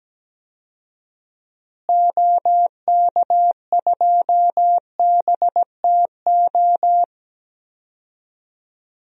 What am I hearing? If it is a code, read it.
Morse code, heard as a keyed tone, about 17 wpm, 700 Hz. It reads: OK2BTO
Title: DXCC-100 Call Signs - 6 Characters - Repeated 2x - 17wpm